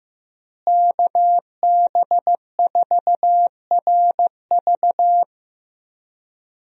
Morse code, heard as a keyed tone, about 15 wpm, 700 Hz. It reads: KB4RV
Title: USA Call Signs - 5 Characters - Repeated 3x - 15wpm